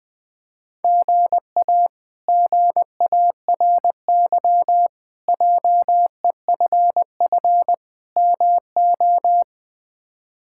Morse code, heard as a keyed tone, about 20 wpm, 700 Hz. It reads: GA GARY JEFF MO
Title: K1USN SST QSO Practice - 20wpm